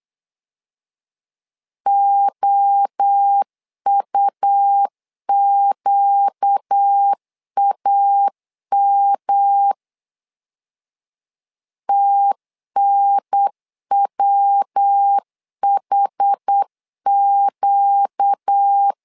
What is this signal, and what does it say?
Morse code, heard as a keyed tone, about 8 wpm, 800 Hz. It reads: OUQAM TNWHQ
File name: 4 Q 8wpm